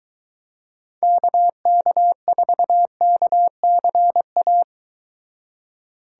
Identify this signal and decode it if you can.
Morse code, heard as a keyed tone, about 23 wpm, 700 Hz. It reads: KK4KCA